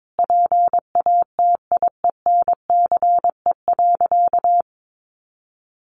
Morse code, heard as a keyed tone, about 22 wpm, 700 Hz. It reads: PATIENCE.
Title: Sentences from Top 1300 Words 22wpm